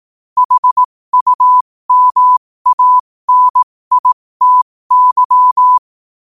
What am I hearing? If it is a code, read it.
Morse code, heard as a keyed tone, about 18 wpm, spaced as for 15 wpm, 1000 Hz. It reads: HUMANITY
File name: qotdfiles / 15wpm